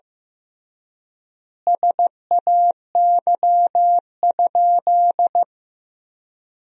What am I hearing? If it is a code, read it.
Morse code, heard as a keyed tone, about 15 wpm, 700 Hz. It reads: SAY?
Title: 3x Word-Spacing - Sentences from Top 500 Words - 15wpm